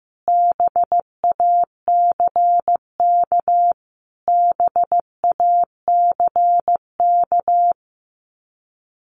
Morse code, heard as a keyed tone, about 15 wpm, 700 Hz. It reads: BACK BACK
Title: Top 200 Words Repeated 15wpm